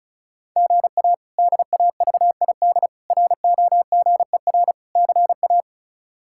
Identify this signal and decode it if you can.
Morse code, heard as a keyed tone, about 36 wpm, 700 Hz. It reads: GA DAVID ROGER CA
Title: K1USN SST QSO Practice - 35wpm